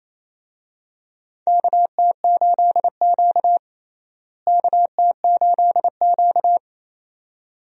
Morse code, heard as a keyed tone, about 28 wpm, 700 Hz. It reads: KT8Q KT8Q
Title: USA Call Signs - 4 Characters - Repeated 2x - 28wpm